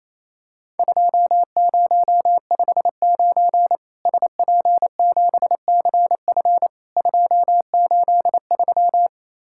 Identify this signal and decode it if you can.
Morse code, heard as a keyed tone, about 28 wpm, 700 Hz. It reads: 2059 SP7CF 283
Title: Worked All Europe DX Contest - QTC Exchange - 28wpm